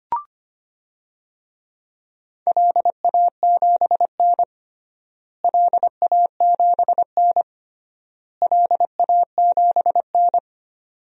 Morse code, heard as a keyed tone, about 25 wpm, 700 Hz. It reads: LA7N LA7N LA7N
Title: DXCC-100 Call Signs - 4 Characters - Repeated 3x - 25wpm